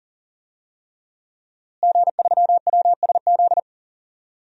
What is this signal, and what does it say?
Morse code, heard as a keyed tone, about 39 wpm, 700 Hz. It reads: G3WSZ
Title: DXCC-100 Call Signs - 5 Characters - Repeated 3x - 40wpm